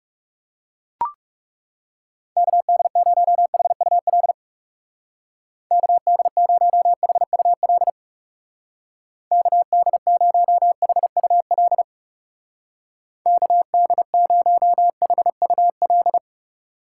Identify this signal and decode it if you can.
Morse code, heard as a keyed tone, about 35 wpm, 700 Hz. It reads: KD0HUL KD0HUL KD0HUL KD0HUL
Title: Speed Racing - USA 2x3 Call Signs 30wpm